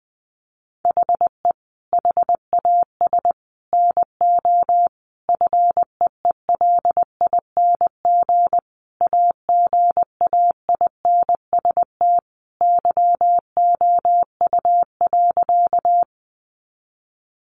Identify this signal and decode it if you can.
Morse code, heard as a keyed tone, about 20 wpm, 700 Hz. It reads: HE HAS NO FEELING AGAINST YOU.
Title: Sentences from Top 1200 Words 20wpm